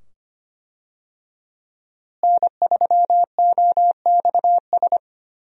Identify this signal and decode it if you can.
Morse code, heard as a keyed tone, about 25 wpm, 700 Hz. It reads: N3OXS
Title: USA Call Signs - 5 Characters - Repeated 2x - 25wpm